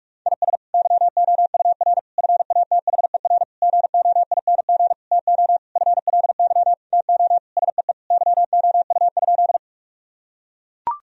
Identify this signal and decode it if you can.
Morse code, heard as a keyed tone, about 42 wpm, 700 Hz. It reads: IS YOUR FATHER GOING TO FLY TO SEE YOU?
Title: Sentences from Top 1000 Words - Review Part 2 - 45wpm